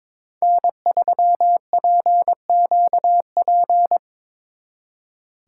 Morse code, heard as a keyed tone, about 26 wpm, 700 Hz. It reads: N3PQP
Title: Speed Racing - USA Call Signs 22wpm